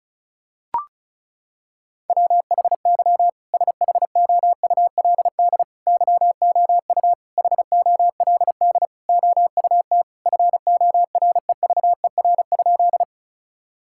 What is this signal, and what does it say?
Morse code, heard as a keyed tone, about 36 wpm, 700 Hz. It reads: WHY SHOULD YOU HOLD OUT FOREVER?
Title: Sentences from Top 1600 Words 35wpm